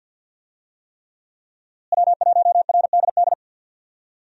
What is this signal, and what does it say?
Morse code, heard as a keyed tone, about 47 wpm, 700 Hz. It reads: W1RDD
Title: USA Call Signs - 5 Characters - Repeated 3x - 50wpm